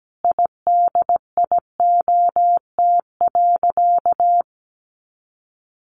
Morse code, heard as a keyed tone, about 17 wpm, 700 Hz. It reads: IDIOT.
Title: A Sound Of Thunder 17wpm